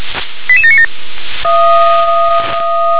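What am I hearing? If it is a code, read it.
Five-tone selective call (ZVEI), heard as quick runs of five tones, tones of about 70 ms. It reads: 88797
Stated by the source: Five-tone ZVEI selective call received by SDR.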